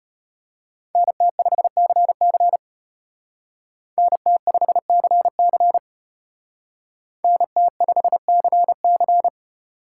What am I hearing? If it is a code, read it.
Morse code, heard as a keyed tone, about 31 wpm, 700 Hz. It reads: NT5CC NT5CC NT5CC
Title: Speed Racing - USA 2x2 Call Signs 25wpm